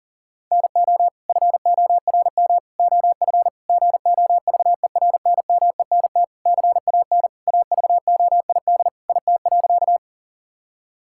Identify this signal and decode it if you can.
Morse code, heard as a keyed tone, about 40 wpm, 700 Hz. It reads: NO FORM OF GOVERNMENT CAN AVOID IT.
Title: Sentences from Top 1000 Words - Review Part 6 - 40wpm